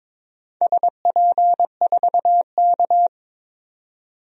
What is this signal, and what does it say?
Morse code, heard as a keyed tone, about 22 wpm, 700 Hz. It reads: SP4K
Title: DXCC-100 Call Signs - 4 Characters - 22wpm